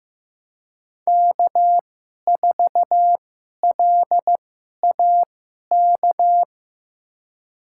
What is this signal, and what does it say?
Morse code, heard as a keyed tone, about 15 wpm, spaced as for 11 wpm, 700 Hz. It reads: K4LAK
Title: USA Call Signs - 5 Characters - 2x Character Spacing - 15wpm